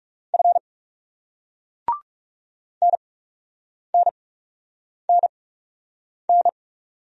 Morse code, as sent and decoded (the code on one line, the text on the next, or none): ..-. / -. -. -. -.
F NNNN